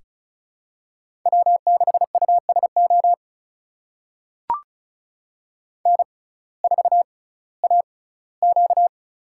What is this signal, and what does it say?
Morse code, heard as a keyed tone, about 36 wpm, spaced as for 19 wpm, 700 Hz. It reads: W6USO N4AQ
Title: USA Call Signs - 6x Character Spacing - 35wpm